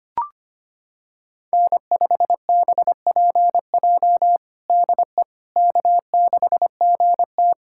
Morse code, heard as a keyed tone, about 25 wpm, 700 Hz. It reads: N5BPJ DE K6GT
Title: K1USN SST QSO Practice 25wpm